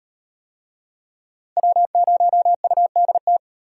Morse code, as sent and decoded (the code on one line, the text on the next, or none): .-- ----- ..- -.. -
W0UDT